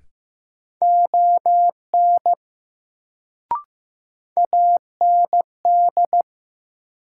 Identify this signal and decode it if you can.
Morse code, heard as a keyed tone, about 15 wpm, 700 Hz. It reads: ON AND
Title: Sets of 2 Words - New Word Drill - 15wpm